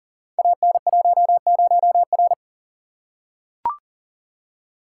Morse code, heard as a keyed tone, about 38 wpm, 700 Hz. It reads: AN10R
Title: Code Groups - 5 Letter:Numbers - 40wpm